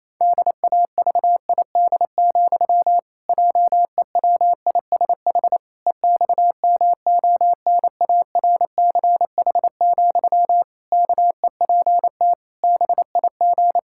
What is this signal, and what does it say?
Morse code, heard as a keyed tone, about 28 wpm, 700 Hz. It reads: DAVID, JEWISH EXMONARCH, KEPT BIG
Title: Pangram Challenge v2 - 28wpm